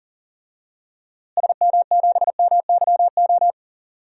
Morse code, heard as a keyed tone, about 40 wpm, 700 Hz. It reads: SM7MYO
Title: DXCC-100 Call Signs - 6 Characters - Repeated 2x - 40wpm